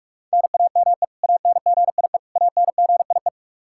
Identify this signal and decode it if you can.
Morse code, heard as a keyed tone, about 46 wpm, 700 Hz. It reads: NAME ANGIE ANGIE
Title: Practice QSO - Exchange 1 - Beginning - 45wpm